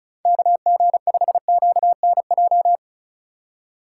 Morse code, heard as a keyed tone, about 36 wpm, 700 Hz. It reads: KG5QNJ
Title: USA Call Signs - 6 Characters - 4x Character Spacing - 35wpm